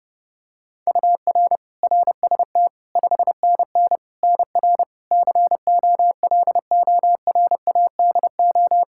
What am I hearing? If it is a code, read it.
Morse code, heard as a keyed tone, about 30 wpm, 700 Hz. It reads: UR RST 5NN NR COLORADO
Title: Practice QSO - Exchange 1 - Beginning - 30wpm